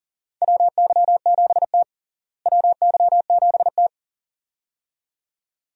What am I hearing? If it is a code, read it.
Morse code, heard as a keyed tone, about 40 wpm, 700 Hz. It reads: WY7T WY7T
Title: USA Call Signs - 4 Characters - Repeated 2x - 40wpm